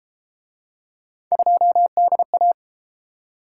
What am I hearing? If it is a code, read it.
Morse code, heard as a keyed tone, about 33 wpm, 700 Hz. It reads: 2DA